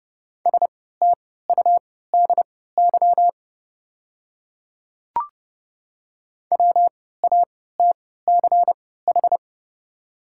Morse code, heard as a keyed tone, about 30 wpm, spaced as for 17 wpm, 690 Hz. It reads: STUDY WATCH
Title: 3x Character Spacing - Top 300 Words - 30wpm